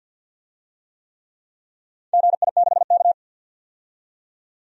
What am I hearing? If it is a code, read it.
Morse code, heard as a keyed tone, about 50 wpm, 700 Hz. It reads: GI6K
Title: DXCC-100 Call Signs - 4 Characters - 50wpm